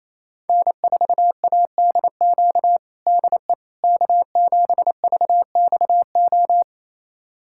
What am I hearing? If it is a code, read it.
Morse code, heard as a keyed tone, about 27 wpm, 700 Hz. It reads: N4ADQ DE K7VXO
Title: USA Callsign DE Callsign - 5 Characters - 28wpm